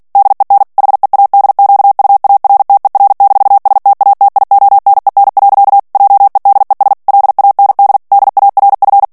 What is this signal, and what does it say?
Morse code, heard as a keyed tone, about 48 wpm, 800 Hz. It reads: DEN HEADQUARTER-STATIONEN. JEDES LAND DARF
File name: morse mp3 / ALLEMAND texte clair 2 vitesse 45 mots minute